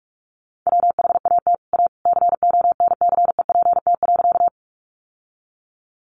Morse code, heard as a keyed tone, about 44 wpm, 700 Hz. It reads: WHAT A CONCEPT.
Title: Sentences from Top 1100 Words 45wpm